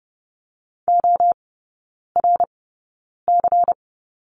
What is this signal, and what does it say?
Morse code, heard as a keyed tone, about 30 wpm, spaced as for 9 wpm, 700 Hz. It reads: ORC
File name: Lesson 18e - Letter m - Three Characters 30wpm